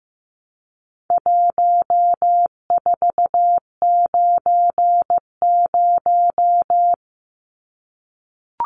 Morse code, heard as a keyed tone, about 15 wpm, 700 Hz. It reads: 1490
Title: Numbers - 4-Digit 15wpm